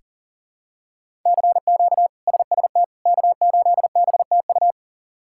Morse code, heard as a keyed tone, about 40 wpm, 700 Hz. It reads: CQ SST K8BTU
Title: K1USN SST QSO Practice - 40wpm